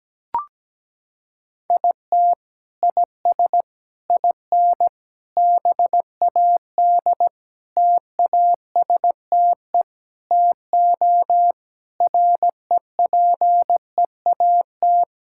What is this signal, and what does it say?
Morse code, heard as a keyed tone, about 17 wpm, 700 Hz. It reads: IT IS IN BAD TASTE TO REPEAT
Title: Sentences from Top 1600 Words 17wpm